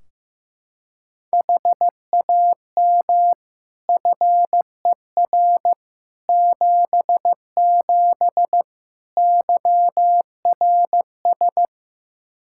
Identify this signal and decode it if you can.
Morse code, heard as a keyed tone, about 16 wpm, 700 Hz. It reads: HAM FER 77 YRS
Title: K1USN SST QSO Practice 15wpm